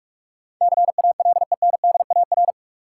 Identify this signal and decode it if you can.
Morse code, heard as a keyed tone, about 46 wpm, 700 Hz. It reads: CALENDAR
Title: Sentences from Top 1600 Words - New Word Drill Alt - 45wpm